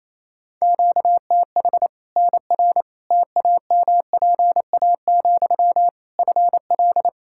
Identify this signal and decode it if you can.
Morse code, heard as a keyed tone, about 27 wpm, 700 Hz. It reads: QTH NR TAMPA, FL
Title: K1USN SST QSO Practice 28wpm